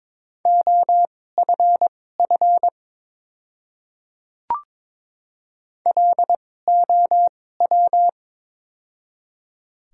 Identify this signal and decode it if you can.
Morse code, heard as a keyed tone, about 22 wpm, spaced as for 16 wpm, 700 Hz. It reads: OFF LOW
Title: 2x Character Spacing - Top 200 Words - 22wpm